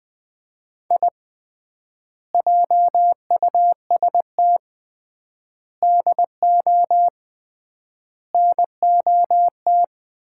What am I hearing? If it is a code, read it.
Morse code, heard as a keyed tone, about 21 wpm, 700 Hz. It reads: I JUST DO NOT
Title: 3x Word-Spacing - Sets of 4 Words - 20wpm